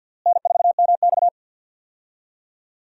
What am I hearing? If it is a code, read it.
Morse code, heard as a keyed tone, about 48 wpm, 700 Hz. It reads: N4MX
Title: USA Call Signs - 4 Characters - Repeated 3x - 50wpm